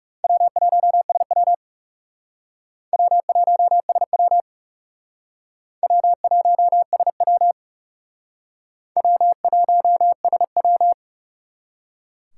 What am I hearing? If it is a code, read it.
Morse code, heard as a keyed tone, about 39 wpm, 700 Hz. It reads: W1SW W1SW W1SW W1SW